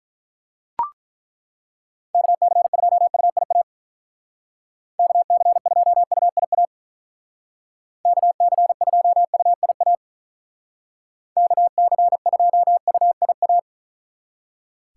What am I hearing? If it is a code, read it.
Morse code, heard as a keyed tone, about 43 wpm, 700 Hz. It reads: KC2UIA KC2UIA KC2UIA KC2UIA
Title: Speed Racing - USA 2x3 Call Signs 35wpm